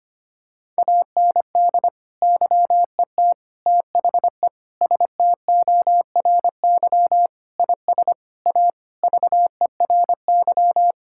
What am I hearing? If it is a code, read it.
Morse code, heard as a keyed tone, about 25 wpm, 700 Hz. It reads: AND YET THE STORY IS A VERY
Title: Sentences from Top 1000 Words - Review Part 6 - 25wpm